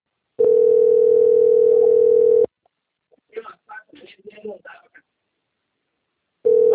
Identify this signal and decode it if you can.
Ringback tone (North America), 440 + 480 Hz, with speech.
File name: out- - - - - .13299